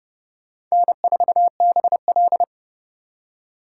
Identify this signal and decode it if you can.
Morse code, heard as a keyed tone, about 30 wpm, 700 Hz. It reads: N4BL
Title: USA Call Signs - 4 Characters - 4x Character Spacing - 30wpm